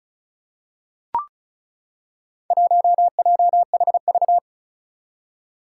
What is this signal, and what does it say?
Morse code, heard as a keyed tone, about 36 wpm, 700 Hz. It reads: 1JHV
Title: Four Challenging Letter-Number Combinations 35wpm